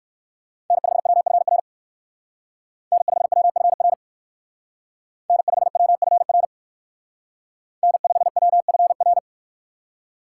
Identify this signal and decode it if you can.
Morse code, heard as a keyed tone, about 53 wpm, 700 Hz. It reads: N5WFR N5WFR N5WFR N5WFR
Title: Speed Racing - USA Call Signs 45wpm